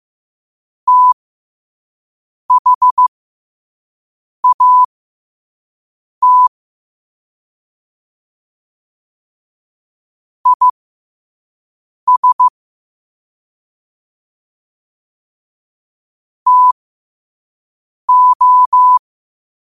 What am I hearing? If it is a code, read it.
Morse code, heard as a keyed tone, about 15 wpm, spaced as for 5 wpm, 1000 Hz. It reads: THAT IS TO
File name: qotdfiles / 5wpm